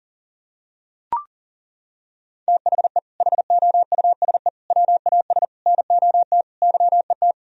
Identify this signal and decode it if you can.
Morse code, heard as a keyed tone, about 37 wpm, 700 Hz. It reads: THE HOUSE WAS NOT YET
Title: Sentences from Top 1000 Words - Review Part 2 - 40wpm